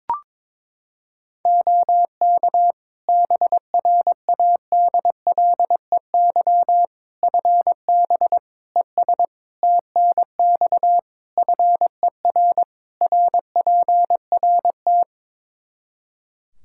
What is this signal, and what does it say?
Morse code, heard as a keyed tone, about 22 wpm, 700 Hz. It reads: OK BRADLEY FB ES TNX FER RPRT
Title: Practice QSO - Exchange 2 - Info - 22wpm